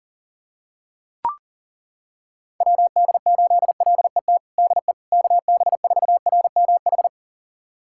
Morse code, heard as a keyed tone, about 40 wpm, 700 Hz. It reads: WD8LET DE KB4RMH
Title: USA Callsign DE Callsign 40wpm